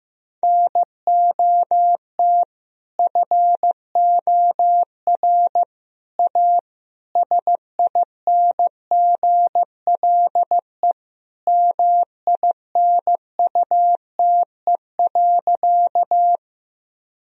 Morse code, heard as a keyed tone, about 15 wpm, 700 Hz. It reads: NOT FOR A SINGLE MINUTE.
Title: Sentences from Top 1000 Words - Review Part 4 - 15wpm